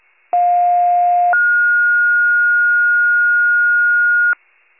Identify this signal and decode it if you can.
Two-tone page, heard as a one-second tone, then a three-second tone, about 710 then 1430 Hz. It is heard from 0.3 to 4.3 s.